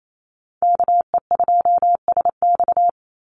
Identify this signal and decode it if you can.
Morse code, heard as a keyed tone, about 28 wpm, 700 Hz. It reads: KE2SX
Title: Call Signs - 28wpm